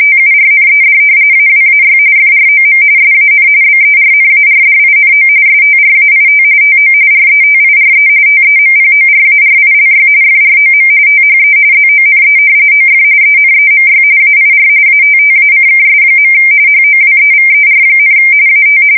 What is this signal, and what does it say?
RTTY, about 45.45 baud. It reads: TURNED AWAY FROM SAN BERNARDINO STRAIT TO RETIRE TEMPORARILY FROM THE ZONE OF ENEMY AIR ATTACKS. HOWEVER, AT
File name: fsk170 / battleshipnj